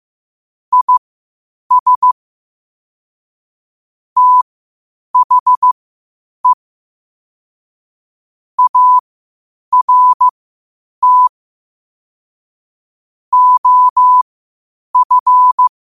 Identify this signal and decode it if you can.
Morse code, heard as a keyed tone, about 15 wpm, spaced as for 8 wpm, 1000 Hz. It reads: IS THE ART OF
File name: qotdfiles / 8wpm